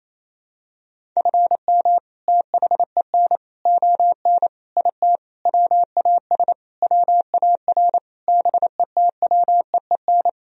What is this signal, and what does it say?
Morse code, heard as a keyed tone, about 27 wpm, 700 Hz. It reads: FM THEN ON IT WAS WAR BETWEEN
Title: Sentences from Top 400 Words and Abbreviations 28wpm